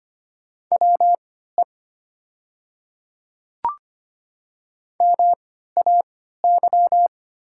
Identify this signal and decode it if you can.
Morse code, heard as a keyed tone, about 25 wpm, spaced as for 14 wpm, 700 Hz. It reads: WE MAY
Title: 3x Character Spacing - Top 200 Words - 25wpm